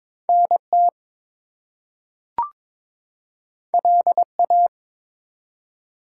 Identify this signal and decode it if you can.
Morse code, heard as a keyed tone, about 22 wpm, 700 Hz. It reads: NT LA
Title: US States and Canadian Provinces 22wpm